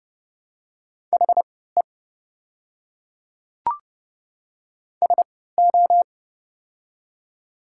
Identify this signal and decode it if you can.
Morse code, heard as a keyed tone, about 30 wpm, spaced as for 17 wpm, 700 Hz. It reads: HE SO
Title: 3x Character Spacing - Top 100 Words - 30wpm